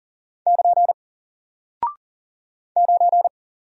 Morse code, heard as a keyed tone, about 40 wpm, 700 Hz. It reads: <KN> 9
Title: Single Character (Letter:Number:Punctuation:Prosign) Alt 40wpm